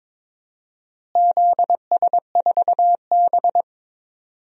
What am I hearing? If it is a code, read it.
Morse code, heard as a keyed tone, about 22 wpm, 700 Hz. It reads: ZS4B
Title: DXCC-100 Call Signs - 4 Characters - Repeated 2x - 22wpm